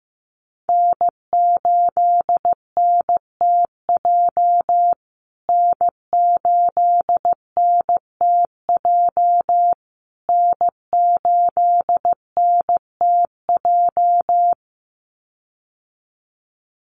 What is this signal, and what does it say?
Morse code, heard as a keyed tone, about 15 wpm, 700 Hz. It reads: N8NTJ N8NTJ N8NTJ
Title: USA Call Signs Repeated 3x 15wpm